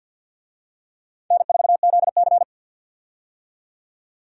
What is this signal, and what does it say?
Morse code, heard as a keyed tone, about 49 wpm, 700 Hz. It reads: N4ZC
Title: USA Call Signs - 4 Characters - 50wpm